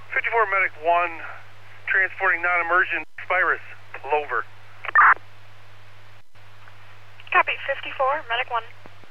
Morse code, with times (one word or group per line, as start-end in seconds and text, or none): none